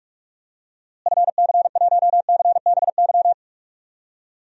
Morse code, heard as a keyed tone, about 46 wpm, 700 Hz. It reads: FC1CBY
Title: DXCC-100 Call Signs - 6 Characters - Repeated 2x - 45wpm